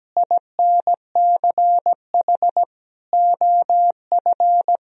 Morse code, heard as a keyed tone, about 17 wpm, 700 Hz. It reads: INCH OF
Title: Sentences from Top 1000 Words - Review Part 3 - 17wpm